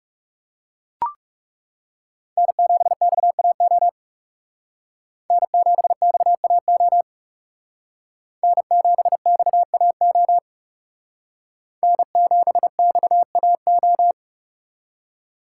Morse code, heard as a keyed tone, about 38 wpm, 700 Hz. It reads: N7XAO N7XAO N7XAO N7XAO